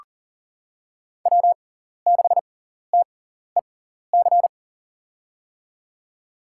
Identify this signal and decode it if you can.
Morse code, heard as a keyed tone, about 40 wpm, spaced as for 14 wpm, 700 Hz. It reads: W6TEC
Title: USA Call Signs - 5 Characters - 6x Character Spacing - 40wpm